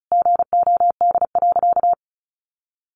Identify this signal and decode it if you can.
Morse code, heard as a keyed tone, about 34 wpm, 700 Hz. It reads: GOD.
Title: A Sound Of Thunder 35wpm